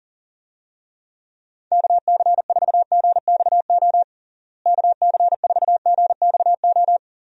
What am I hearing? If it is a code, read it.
Morse code, heard as a keyed tone, about 40 wpm, 700 Hz. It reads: KC4GXO KC4GXO
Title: USA Call Signs - 6 Characters - Repeated 3x - 40wpm